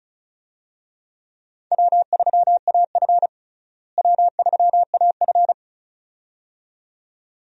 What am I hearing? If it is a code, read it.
Morse code, heard as a keyed tone, about 34 wpm, 700 Hz. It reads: W3AF W3AF
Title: USA Call Signs - 4 Characters - Repeated 2x - 35wpm